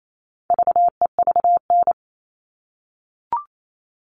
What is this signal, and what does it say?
Morse code, heard as a keyed tone, about 28 wpm, 700 Hz. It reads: VEVN
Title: All Cut Numbers - 4-Digit 28wpm